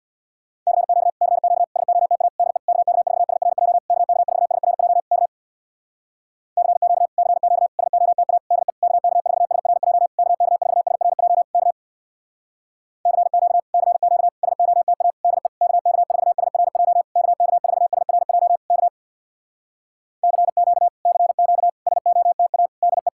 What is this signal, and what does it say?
Morse code, heard as a keyed tone, about 66 wpm, 700 Hz. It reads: CQ CQ SOTA DE KK4SRJ KK4SRJ K CQ CQ SOTA DE KK4SRJ KK4SRJ K CQ CQ SOTA DE KK4SRJ KK4SRJ K CQ CQ SOTA DE